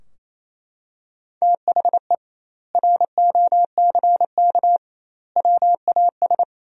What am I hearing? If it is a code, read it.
Morse code, heard as a keyed tone, about 28 wpm, 700 Hz. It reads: THE ROCK WAS
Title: 2x Word-Spacing - Sentences from Top 400 Words - 28wpm